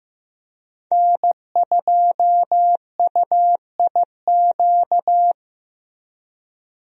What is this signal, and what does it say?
Morse code, heard as a keyed tone, about 15 wpm, 700 Hz. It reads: N2UIQ
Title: USA Call Signs - 5 Characters - Repeated 2x - 15wpm